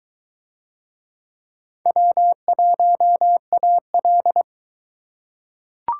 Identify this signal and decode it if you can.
Morse code, heard as a keyed tone, about 23 wpm, 700 Hz. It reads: W1AL